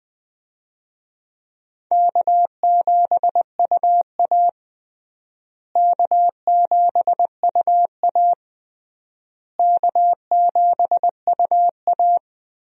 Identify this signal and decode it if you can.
Morse code, heard as a keyed tone, about 20 wpm, 700 Hz. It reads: K7UA K7UA K7UA
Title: USA Call Signs - 4 Characters - Repeated 3x - 20wpm